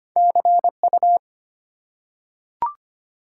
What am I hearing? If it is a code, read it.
Morse code, heard as a keyed tone, about 25 wpm, 700 Hz. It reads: CU
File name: Top 90 QSO Elements - 3x Character Spacing - 25wpm